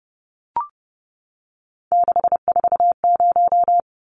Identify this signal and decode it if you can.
Morse code, heard as a keyed tone, about 30 wpm, 700 Hz. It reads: 640
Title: Numbers - 3-Digit 30wpm